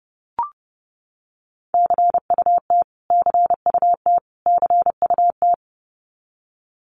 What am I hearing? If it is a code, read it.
Morse code, heard as a keyed tone, about 30 wpm, 700 Hz. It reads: CUT CUT CUT
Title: Top 500 Words Repeated 30wpm